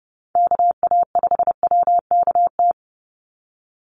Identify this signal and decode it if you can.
Morse code, heard as a keyed tone, about 28 wpm, 700 Hz. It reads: KA5WKT
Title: USA Call Signs 30wpm